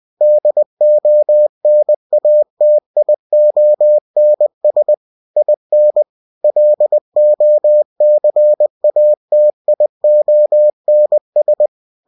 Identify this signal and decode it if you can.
Morse code, heard as a keyed tone, about 20 wpm, 600 Hz. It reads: DONATIONS IN LOCATIONS
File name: cw-books / winnie-the-pooh / winnie-20wpm-page-0121